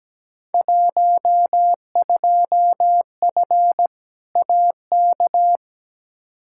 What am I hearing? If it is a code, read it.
Morse code, heard as a keyed tone, about 17 wpm, 700 Hz. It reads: 12F AK
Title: Field Day Exchanges Encore - 17wpm